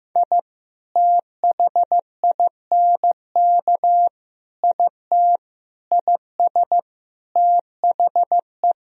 Morse code, heard as a keyed tone, about 15 wpm, 700 Hz. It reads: I THINK IT IS THE
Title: Sentences from Top 1000 Words - Review Part 2 - 15wpm